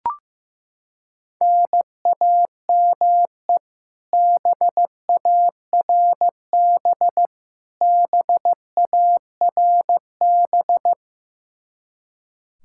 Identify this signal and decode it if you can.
Morse code, heard as a keyed tone, about 15 wpm, 700 Hz. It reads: NAME BARB BARB
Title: Practice QSO - Exchange 1 - Beginning - 15wpm